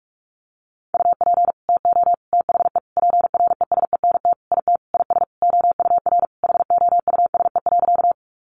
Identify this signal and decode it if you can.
Morse code, heard as a keyed tone, about 44 wpm, 700 Hz. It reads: UP TO THE PRESENT IT IS OUR HOUSE.
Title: Sentences from Top 800 Words 45wpm